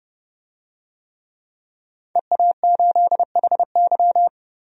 Morse code, heard as a keyed tone, about 30 wpm, 700 Hz. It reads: EA8HY
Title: DXCC-100 Call Signs - 5 Characters - 30wpm